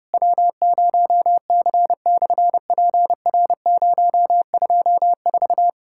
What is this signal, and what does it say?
Morse code, heard as a keyed tone, about 30 wpm, 690 Hz. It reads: W0C/PR024
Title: SOTA QSO Practice 30wpm